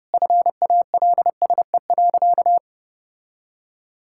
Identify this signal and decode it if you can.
Morse code, heard as a keyed tone, about 30 wpm, 700 Hz. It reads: FALSE.
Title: Sentences from Top 1600 Words 30wpm